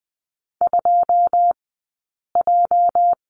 Morse code, heard as a keyed tone, about 20 wpm, spaced as for 8 wpm, 700 Hz. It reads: 2J
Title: 211 Lesson 34d - Number 8 - Two Characters 20wpm